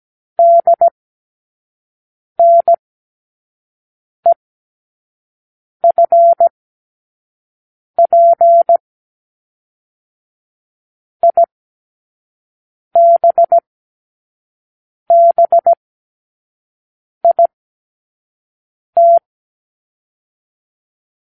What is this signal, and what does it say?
Morse code, heard as a keyed tone, about 17 wpm, spaced as for 5 wpm, 680 Hz. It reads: DNEFP IBBIT
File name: k7qo / Lesson 36 (A-X)